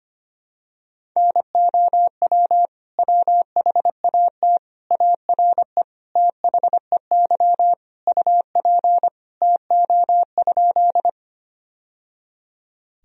Morse code, heard as a keyed tone, about 25 wpm, 700 Hz. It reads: NOW WHAT ARE THEY UP TO?
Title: Sentences from Top 1000 Words - Review Part 1 - 25wpm